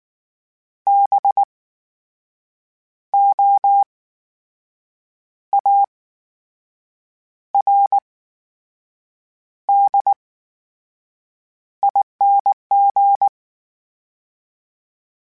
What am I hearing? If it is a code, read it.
Morse code, heard as a keyed tone, about 19 wpm, spaced as for 6 wpm, 800 Hz. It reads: BOARDING
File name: Special INGs / Special INGs-mp3s / VerySlowToFaster Ing @18